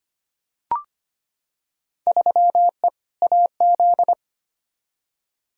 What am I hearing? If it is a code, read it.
Morse code, heard as a keyed tone, about 25 wpm, 700 Hz. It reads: 3E AZ
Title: Field Day Exchanges Encore - 25wpm